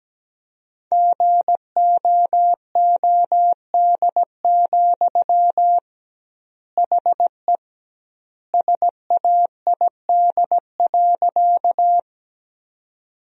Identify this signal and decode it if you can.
Morse code, heard as a keyed tone, about 17 wpm, 700 Hz. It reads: GOOD, HE SAID.
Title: 2x Word-Spacing - Sentences from Top 200 Words - 17wpm